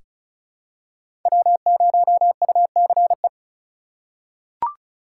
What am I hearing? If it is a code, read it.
Morse code, heard as a keyed tone, about 34 wpm, 700 Hz. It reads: W0UCE
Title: K1USN SST QSO Practice - 35wpm